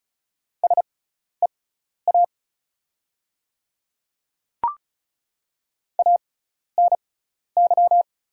Morse code, heard as a keyed tone, about 36 wpm, spaced as for 12 wpm, 690 Hz. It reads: SEA ANY
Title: 6x Character Spacing - Top 500 Words - 35wpm